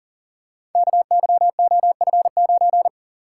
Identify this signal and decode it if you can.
Morse code, heard as a keyed tone, about 40 wpm, 700 Hz. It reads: KYOF9
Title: Code Groups - 5 Letter:Numbers - 40wpm